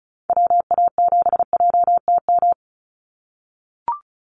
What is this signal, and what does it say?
Morse code, heard as a keyed tone, about 34 wpm, 700 Hz. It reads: WA7JTM
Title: NA SOTA Chaser Callsigns 35wpm